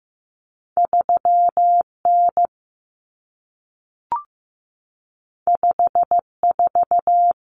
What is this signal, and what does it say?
Morse code, heard as a keyed tone, about 15 wpm, 700 Hz. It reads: 3 N 5 4
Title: Common Cut Numbers - 2-Digit 15wpm